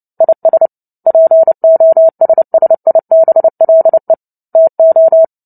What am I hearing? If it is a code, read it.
Morse code, heard as a keyed tone, about 30 wpm, 650 Hz. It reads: IS POSSIBLE TO
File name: Book Two / 03-Chapter 3 - 29 WPM